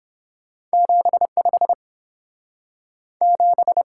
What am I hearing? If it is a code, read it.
Morse code, heard as a keyed tone, about 30 wpm, 700 Hz. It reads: 75 7